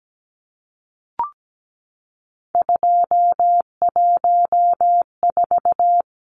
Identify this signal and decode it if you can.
Morse code, heard as a keyed tone, about 17 wpm, 700 Hz. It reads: 214
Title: Numbers - 3-Digit 17wpm